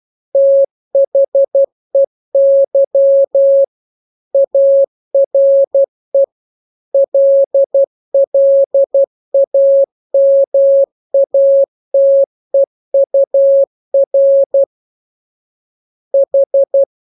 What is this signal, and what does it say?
Morse code, heard as a keyed tone, about 12 wpm, 550 Hz. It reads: THEY ARE LLAMATEUR H
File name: jokes / joke 14 12 0000